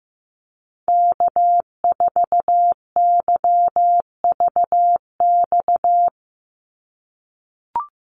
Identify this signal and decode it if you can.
Morse code, heard as a keyed tone, about 15 wpm, 700 Hz. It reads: K4YVX
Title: USA Call Signs 15wpm